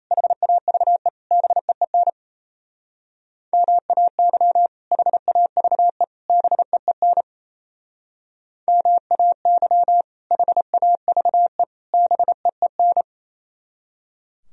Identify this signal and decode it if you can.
Morse code, heard as a keyed tone, about 33 wpm, 700 Hz. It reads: HAVE BEEN MAY HAVE BEEN MAY HAVE BEEN
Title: Speed-Racing - Sets of 3 Words 28wpm